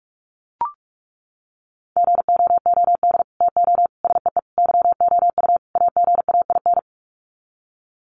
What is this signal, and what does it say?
Morse code, heard as a keyed tone, about 44 wpm, 700 Hz. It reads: GOOD TO SEE YOU AGAIN
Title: Sets of 5 Words Encore2 45wpm